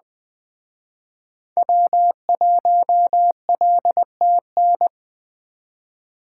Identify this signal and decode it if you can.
Morse code, heard as a keyed tone, about 21 wpm, 700 Hz. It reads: W1LTN
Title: Speed Racing - USA Call Signs 17wpm